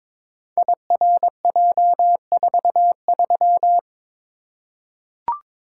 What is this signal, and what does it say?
Morse code, heard as a keyed tone, about 22 wpm, 700 Hz. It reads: IRJ43
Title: Code Groups - 5 Letter:Numbers - 22wpm